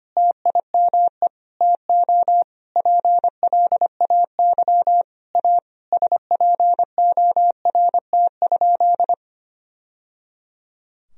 Morse code, heard as a keyed tone, about 25 wpm, 700 Hz. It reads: TIME TO PLAY A SPORT?
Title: Sentences from Top 1000 Words - Review Part 4 - 25wpm